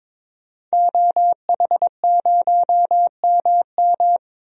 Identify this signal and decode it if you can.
Morse code, heard as a keyed tone, about 22 wpm, 700 Hz. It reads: OH0MM
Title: DXCC-100 Call Signs - 5 Characters - 22wpm